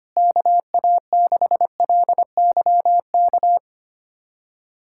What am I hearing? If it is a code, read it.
Morse code, heard as a keyed tone, about 25 wpm, 700 Hz. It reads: KA6LYK
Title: USA Callsign DE Callsign 25wpm